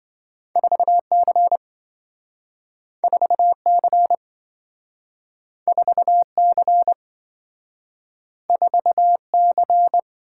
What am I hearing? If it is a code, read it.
Morse code, heard as a keyed tone, about 27 wpm, 700 Hz. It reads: 4C 4C 4C 4C